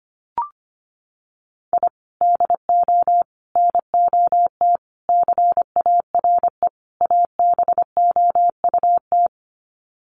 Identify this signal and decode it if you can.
Morse code, heard as a keyed tone, about 25 wpm, 700 Hz. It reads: I DO NOT CARE ABOUT
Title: Sets of 5 Words Encore2 25wpm